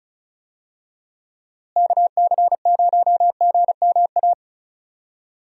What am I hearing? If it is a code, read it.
Morse code, heard as a keyed tone, about 36 wpm, 700 Hz. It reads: KC0GMA
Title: USA Call Signs - 6 Characters - Repeated 2x - 35wpm